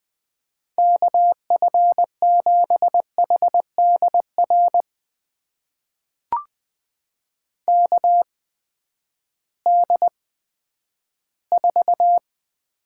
Morse code, heard as a keyed tone, about 20 wpm, spaced as for 11 wpm, 700 Hz. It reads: KF7HDR KD4
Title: USA Call Signs - 6 Characters - 8x Character Spacing - 20wpm